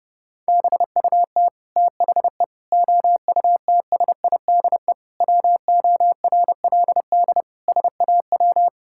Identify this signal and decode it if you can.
Morse code, heard as a keyed tone, about 32 wpm, 700 Hz. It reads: BUT THE OUTSIDE WORLD SAW
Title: Sentences from Top 1000 Words - Review Part 6 - 30wpm